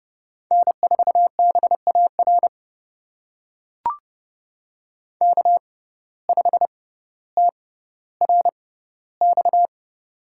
Morse code, heard as a keyed tone, about 30 wpm, spaced as for 15 wpm, 700 Hz. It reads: N4BAR K5TRX
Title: USA Call Signs - 5 Characters - 6x Character Spacing - 30wpm